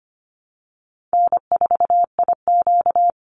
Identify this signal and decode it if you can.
Morse code, heard as a keyed tone, about 25 wpm, 700 Hz. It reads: N4IQ
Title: Easy Call Signs Repeated 3x - 25wpm